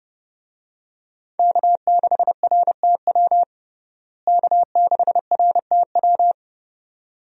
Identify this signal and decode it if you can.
Morse code, heard as a keyed tone, about 32 wpm, 700 Hz. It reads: K6RTW K6RTW
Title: USA Call Signs - 5 Characters - Repeated 2x - 30wpm